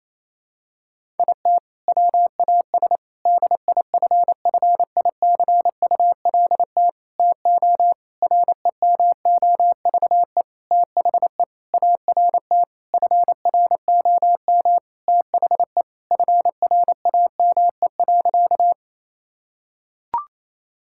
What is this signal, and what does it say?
Morse code, heard as a keyed tone, about 28 wpm, 700 Hz. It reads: IT WAS DIFFICULT TO REMOVE THE ART FROM THE FRAME.
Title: Sentences from Top 1000 Words - Review Part 6 - 28wpm